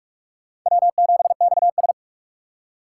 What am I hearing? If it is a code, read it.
Morse code, heard as a keyed tone, about 46 wpm, 700 Hz. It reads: W7XS
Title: USA Call Signs - 4x Character Spacing - 45wpm